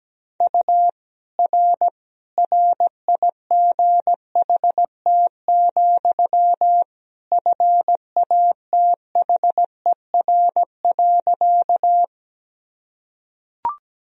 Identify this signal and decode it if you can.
Morse code, heard as a keyed tone, about 17 wpm, 700 Hz. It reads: U R RIGHT, FATHER.
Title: Sentences from Top 300 Words and Abbreviations 17wpm